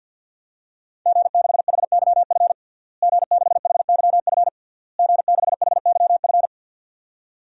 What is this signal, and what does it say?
Morse code, heard as a keyed tone, about 46 wpm, 700 Hz. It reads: G6HYF G6HYF G6HYF
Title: DXCC-100 Call Signs - 5 Characters - Repeated 3x - 50wpm